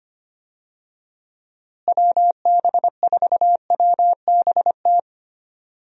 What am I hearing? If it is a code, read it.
Morse code, heard as a keyed tone, about 25 wpm, 700 Hz. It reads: WB4WBT
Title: USA Call Signs - 6 Characters - Repeated 3x - 25wpm